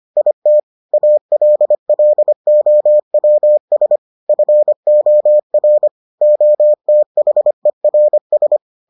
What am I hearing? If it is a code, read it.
Morse code, heard as a keyed tone, about 25 wpm, 600 Hz. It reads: IT ALLOWS FOR OTHERS